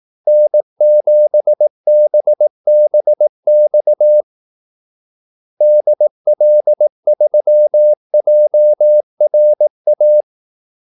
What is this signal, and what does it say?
Morse code, heard as a keyed tone, about 18 wpm, 600 Hz. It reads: N7BBX DL3JRA
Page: Kaldesignaler 18 wpm